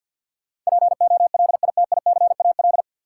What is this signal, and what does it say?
Morse code, heard as a keyed tone, about 52 wpm, 700 Hz. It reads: POLITICAL
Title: Sentences from Top 800 Words - New Word Drill - 50wpm